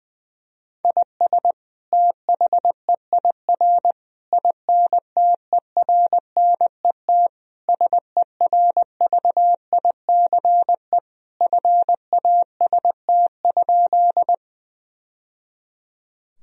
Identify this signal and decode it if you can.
Morse code, heard as a keyed tone, about 20 wpm, 700 Hz. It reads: IS THEIR INTERNET SERVICE FAST?
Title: Sentences from Top 1000 Words - Review Part 4 - 20wpm